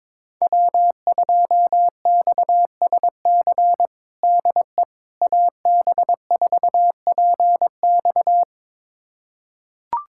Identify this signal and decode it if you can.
Morse code, heard as a keyed tone, about 22 wpm, 700 Hz. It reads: W2XSC DE AB4PX
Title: USA Callsign DE Callsign - 5 Characters - 22wpm